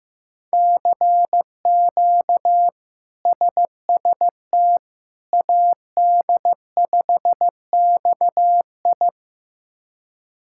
Morse code, heard as a keyed tone, about 15 wpm, 700 Hz. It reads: CQ SST AD5XI
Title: K1USN SST QSO Practice - 15wpm